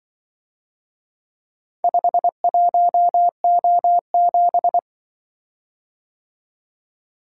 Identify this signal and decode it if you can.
Morse code, heard as a keyed tone, about 24 wpm, 700 Hz. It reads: 51O7
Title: Speed-Racing - Four Letter-Number Combinations - 20wpm